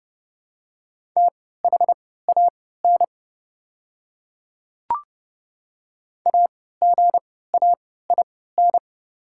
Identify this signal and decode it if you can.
Morse code, heard as a keyed tone, about 30 wpm, spaced as for 17 wpm, 700 Hz. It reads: THAN AGAIN
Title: 3x Character Spacing - Top 200 Words - 30wpm